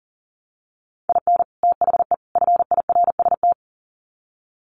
Morse code, heard as a keyed tone, about 40 wpm, 700 Hz. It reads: IN THE FIRST
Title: Sets of 3 Words 40wpm